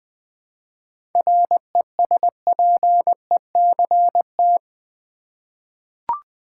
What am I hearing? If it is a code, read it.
Morse code, heard as a keyed tone, about 20 wpm, 700 Hz. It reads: RESPECT
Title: 7 Letter Words - First 3 Letters Given - 20wpm